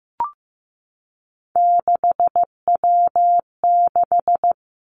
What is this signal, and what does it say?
Morse code, heard as a keyed tone, about 15 wpm, 700 Hz. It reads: 6W6
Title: Three Challenging Letter-Number Combinations 15wpm